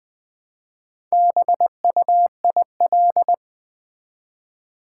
Morse code, heard as a keyed tone, about 20 wpm, 700 Hz. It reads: BUIL
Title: Build A Word - 1 Letter At A Time - 3 to 5-Letter Words - 20wpm